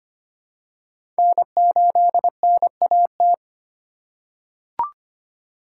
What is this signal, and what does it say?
Morse code, heard as a keyed tone, about 25 wpm, 700 Hz. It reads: N8NAT
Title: USA Call Signs - 5 Characters - 4x Character Spacing - 25wpm